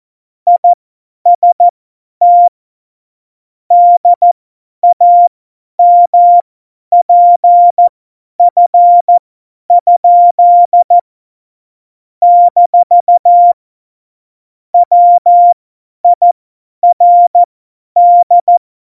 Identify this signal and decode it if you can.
Morse code, heard as a keyed tone, about 14 wpm, spaced as for 10 wpm, 700 Hz. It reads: IST DAMPF? - WIRD
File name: Morsekurs / Witze / Witz 12. November 2019, 50 F70